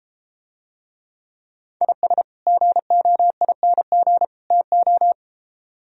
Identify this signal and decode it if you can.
Morse code, heard as a keyed tone, about 33 wpm, 700 Hz. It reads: IS GOING TO